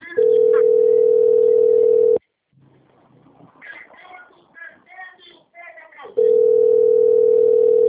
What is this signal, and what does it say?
Ringback tone (North America), 440 + 480 Hz, with speech.
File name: out- - - - - .23289